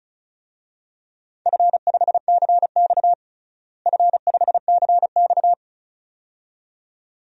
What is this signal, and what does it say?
Morse code, heard as a keyed tone, about 36 wpm, 700 Hz. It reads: F5CX F5CX
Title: DXCC-100 Call Signs - 4 Characters - Repeated 2x - 35wpm